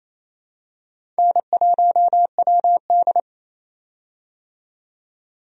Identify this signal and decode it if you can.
Morse code, heard as a keyed tone, about 28 wpm, 700 Hz. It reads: N1WD
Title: USA Call Signs - 4 Characters - 28wpm